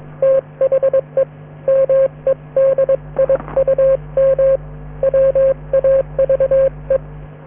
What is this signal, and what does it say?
Morse code, heard as a keyed tone, about 22 wpm, 540 Hz. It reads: THE MEDIUM WAVE